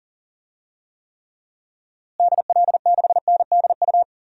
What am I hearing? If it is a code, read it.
Morse code, heard as a keyed tone, about 37 wpm, 700 Hz. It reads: DL6NDU
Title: DXCC-100 Call Signs - 6 Characters - Repeated 2x - 40wpm